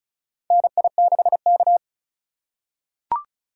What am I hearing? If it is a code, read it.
Morse code, heard as a keyed tone, about 35 wpm, 700 Hz. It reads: NI6K
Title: USA Call Signs - 4 Characters - 2x Character Spacing - 35wpm